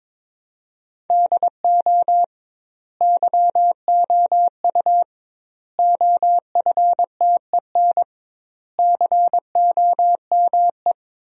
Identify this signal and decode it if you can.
Morse code, heard as a keyed tone, about 22 wpm, 700 Hz. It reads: DO YOU OFTEN COME
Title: 2x Word-Spacing - Sentences from Top 300 Words - 22wpm